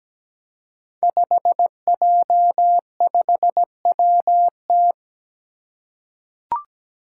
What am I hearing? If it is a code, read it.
Morse code, heard as a keyed tone, about 17 wpm, 700 Hz. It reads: 5J5WT
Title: Five Challenging Letter-Number Combinations 17wpm